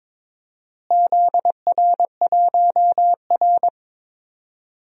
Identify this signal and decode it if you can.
Morse code, heard as a keyed tone, about 22 wpm, 700 Hz. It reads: ZR1R
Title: DXCC-100 Call Signs - 4 Characters - Repeated 3x - 22wpm